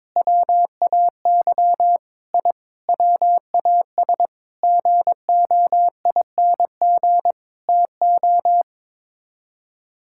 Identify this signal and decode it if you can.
Morse code, heard as a keyed tone, about 22 wpm, 700 Hz. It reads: WAY I WAS GOING TO
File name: practice / 22wpm / Sets of 8 Words - 22wpm